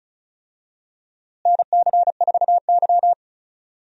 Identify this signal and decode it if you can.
Morse code, heard as a keyed tone, about 36 wpm, 700 Hz. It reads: NC4Y
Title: USA Call Signs - 4 Characters - Repeated 2x - 35wpm